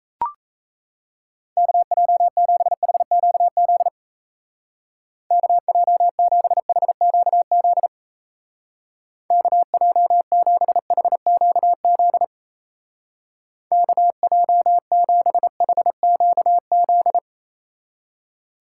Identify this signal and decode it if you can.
Morse code, heard as a keyed tone, about 35 wpm, 700 Hz. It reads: KJ7HQZ KJ7HQZ KJ7HQZ KJ7HQZ
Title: Speed Racing - USA 2x3 Call Signs 28wpm